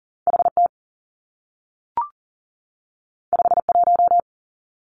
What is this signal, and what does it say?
Morse code, heard as a keyed tone, about 40 wpm, 700 Hz. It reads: HT 51
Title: Two Challenging Letter-Number Combinations 40wpm